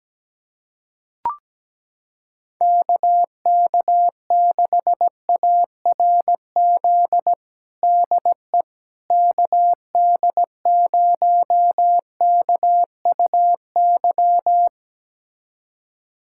Morse code, heard as a keyed tone, about 17 wpm, 700 Hz. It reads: KK6ARZ DE KD0KUY
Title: USA Callsign DE Callsign 17wpm